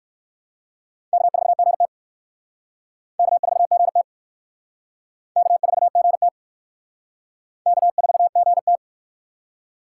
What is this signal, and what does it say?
Morse code, heard as a keyed tone, about 60 wpm, 700 Hz. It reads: K4GT K4GT K4GT K4GT